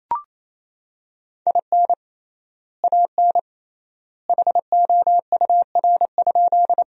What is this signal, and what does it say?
Morse code, heard as a keyed tone, about 28 wpm, 700 Hz. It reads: IN AN HOUR?
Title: 3x Word-Spacing - Sentences from Top 400 Words - 28wpm